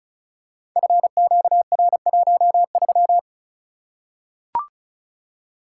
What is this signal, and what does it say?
Morse code, heard as a keyed tone, about 36 wpm, 700 Hz. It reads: FQR13
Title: Code Groups - 5 Letter:Numbers - 35wpm